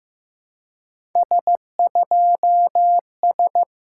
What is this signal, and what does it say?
Morse code, heard as a keyed tone, about 15 wpm, 690 Hz. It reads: S2S
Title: SOTA QSO Practice 15wpm